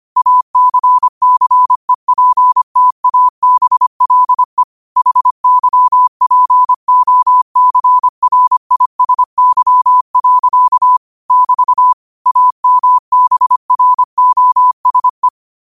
Morse code, heard as a keyed tone, about 25 wpm, 1000 Hz. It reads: ACCEPTABLE HYPOCRISY. <BT> AMBROSE
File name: qotdfiles / 25wpm